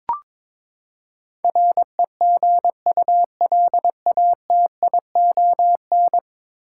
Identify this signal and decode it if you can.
Morse code, heard as a keyed tone, about 22 wpm, 700 Hz. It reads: REGULATION
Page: morse-code-ninja-practice-22wpm